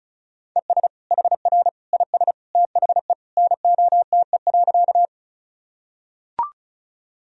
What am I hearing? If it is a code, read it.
Morse code, heard as a keyed tone, about 36 wpm, 700 Hz. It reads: ES HR IS THE NOTE.
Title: Sentences from Top 500 Words and Abbreviations 35wpm